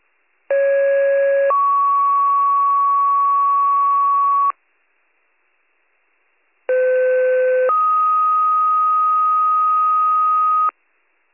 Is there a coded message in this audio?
Yes: two-tone page from 0.5 to 10.7 s, heard as a one-second tone, then a three-second tone, about 550 then 1120 Hz, about 510 then 1250 Hz.